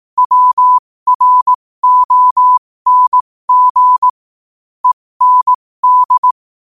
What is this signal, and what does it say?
Morse code, heard as a keyed tone, about 18 wpm, spaced as for 15 wpm, 1000 Hz. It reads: WRONG END
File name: qotdfiles / 15wpm